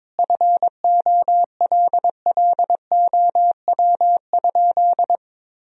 Morse code, heard as a keyed tone, about 22 wpm, 700 Hz. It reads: FOLLOW?
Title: Sentences from Top 1000 Words - Review Part 2 - 22wpm